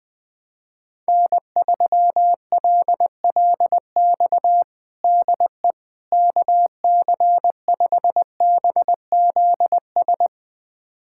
Morse code, heard as a keyed tone, about 20 wpm, 700 Hz. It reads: N3LLX DE KC5BZS
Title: USA Callsign DE Callsign 20wpm